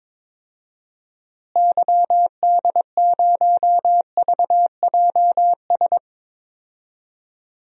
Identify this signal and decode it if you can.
Morse code, heard as a keyed tone, about 22 wpm, 700 Hz. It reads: YD0VJS
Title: DXCC-100 Call Signs - 6 Characters - Repeated 2x - 22wpm